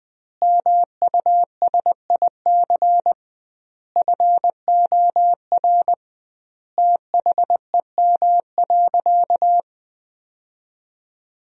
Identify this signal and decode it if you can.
Morse code, heard as a keyed tone, about 20 wpm, 700 Hz. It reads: MUSIC FOR THEM.
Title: 2x Word-Spacing - Sentences from Top 300 Words - 20wpm